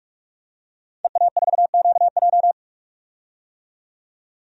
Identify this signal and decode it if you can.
Morse code, heard as a keyed tone, about 46 wpm, 700 Hz. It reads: EA4QJ
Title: DXCC-100 Call Signs - 5 Characters - 45wpm